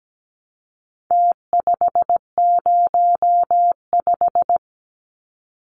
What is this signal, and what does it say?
Morse code, heard as a keyed tone, about 17 wpm, 700 Hz. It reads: T505
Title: Numbers - Contest Style - 4-Digit 17wpm